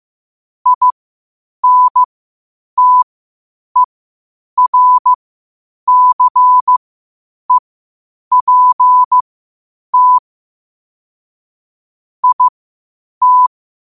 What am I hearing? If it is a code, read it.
Morse code, heard as a keyed tone, about 15 wpm, spaced as for 8 wpm, 1000 Hz. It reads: INTERCEPT IT
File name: qotdfiles / 8wpm